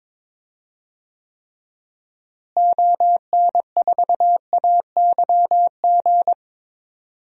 Morse code, heard as a keyed tone, about 22 wpm, 700 Hz. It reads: ON4AYG
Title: DXCC-100 Call Signs - 6 Characters - Repeated 2x - 22wpm